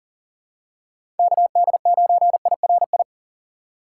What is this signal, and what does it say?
Morse code, heard as a keyed tone, about 40 wpm, 700 Hz. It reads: KD9IRI
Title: USA Call Signs - 6 Characters - 4x Character Spacing - 40wpm